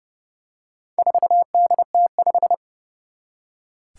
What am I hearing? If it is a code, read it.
Morse code, heard as a keyed tone, about 30 wpm, 700 Hz. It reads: 4DT5
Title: Four Challenging Letter-Number Combinations 30wpm